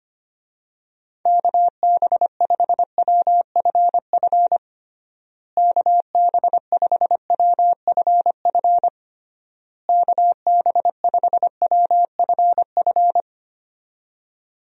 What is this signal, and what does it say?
Morse code, heard as a keyed tone, about 25 wpm, 700 Hz. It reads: KB5WFF KB5WFF KB5WFF
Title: USA Call Signs - 6 Characters - Repeated 3x - 25wpm